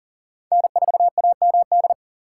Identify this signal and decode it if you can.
Morse code, heard as a keyed tone, about 50 wpm, 690 Hz. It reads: N4AMD
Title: Speed Racing - USA Call Signs 40wpm